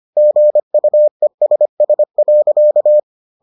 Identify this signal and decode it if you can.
Morse code, heard as a keyed tone, about 25 wpm, 600 Hz. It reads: GUESS.